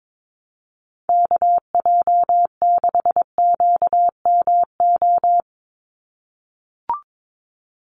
Morse code, heard as a keyed tone, about 22 wpm, 700 Hz. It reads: KJ6QMO
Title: USA Call Signs 22wpm